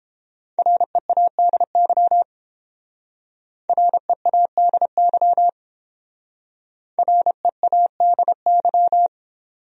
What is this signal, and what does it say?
Morse code, heard as a keyed tone, about 30 wpm, 700 Hz. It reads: READY READY READY